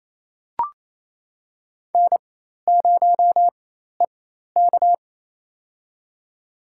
Morse code, heard as a keyed tone, about 28 wpm, spaced as for 13 wpm, 700 Hz. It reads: N0EK
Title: USA Call Signs - 4 Characters - 4x Character Spacing - 28wpm